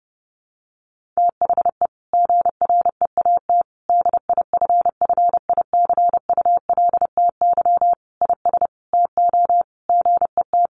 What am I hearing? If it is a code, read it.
Morse code, heard as a keyed tone, about 30 wpm, 700 Hz. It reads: THE GREAT DIFFICULTY IS TO GET
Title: Sentences from Top 1500 Words 30wpm